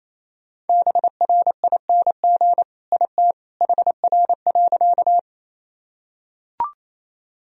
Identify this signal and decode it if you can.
Morse code, heard as a keyed tone, about 28 wpm, 700 Hz. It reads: BRING IT HR.
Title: Sentences from Top 500 Words and Abbreviations 28wpm